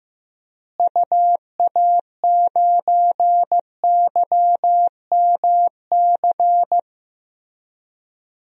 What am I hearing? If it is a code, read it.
Morse code, heard as a keyed tone, about 15 wpm, 700 Hz. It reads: UA9YMC
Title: DXCC-100 Call Signs - 6 Characters - Repeated 2x - 15wpm